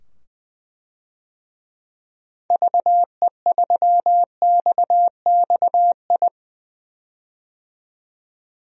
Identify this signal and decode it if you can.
Morse code, heard as a keyed tone, about 21 wpm, 700 Hz. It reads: VE3XXI
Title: DXCC-100 Call Signs - 6 Characters - 20wpm